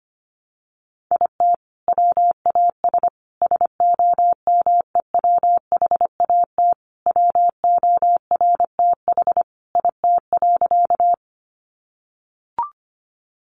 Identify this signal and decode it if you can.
Morse code, heard as a keyed tone, about 25 wpm, 700 Hz. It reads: IT WAS SOMEWHAT WORTH IT.
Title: Sentences from Top 1300 Words 25wpm